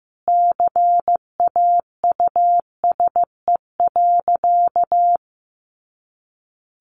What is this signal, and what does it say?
Morse code, heard as a keyed tone, about 15 wpm, 700 Hz. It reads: CAUSE.
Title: Sentences from Top 1400 Words 15wpm